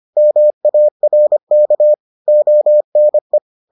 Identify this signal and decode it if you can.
Morse code, heard as a keyed tone, about 25 wpm, 600 Hz. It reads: MARK ONE